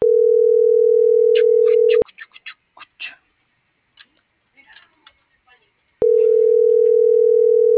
Ringback tone (North America), 440 + 480 Hz.